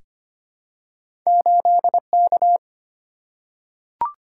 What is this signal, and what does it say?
Morse code, heard as a keyed tone, about 25 wpm, 700 Hz. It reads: 8K
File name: Two Letter-Number Combinations - 25wpm